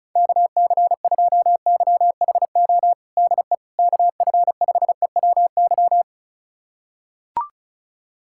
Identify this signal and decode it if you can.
Morse code, heard as a keyed tone, about 36 wpm, 700 Hz. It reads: KC2YHO DE KF5EWY
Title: USA Callsign DE Callsign - 6 Characters - 35wpm